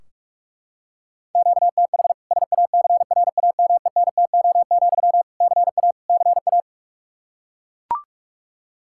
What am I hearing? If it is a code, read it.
Morse code, heard as a keyed tone, about 41 wpm, 700 Hz. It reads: QTH SACRAMENTO, CA CA
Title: K1USN SST QSO Practice 45wpm